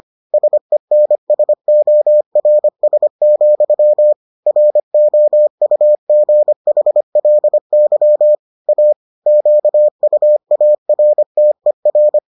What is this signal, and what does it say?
Morse code, heard as a keyed tone, about 25 wpm, 600 Hz. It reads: SENSORS, ROUGHLY A QUARTER